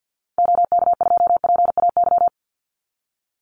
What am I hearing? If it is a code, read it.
Morse code, heard as a keyed tone, about 50 wpm, 700 Hz. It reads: OK2PAY
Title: Call Signs - 50wpm